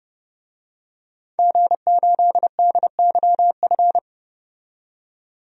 Morse code, heard as a keyed tone, about 29 wpm, 700 Hz. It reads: G8DYF
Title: DXCC-100 Call Signs - 5 Characters - 30wpm